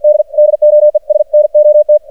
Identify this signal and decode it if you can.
Morse code, heard as a keyed tone, about 52 wpm, 600 Hz. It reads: D DOES NOT
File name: archived / CRSnet